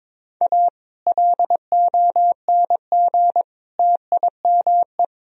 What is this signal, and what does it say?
Morse code, heard as a keyed tone, about 22 wpm, 700 Hz. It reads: A LONG TIME
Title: Sentences from Top 1000 Words - Review Part 3 - 22wpm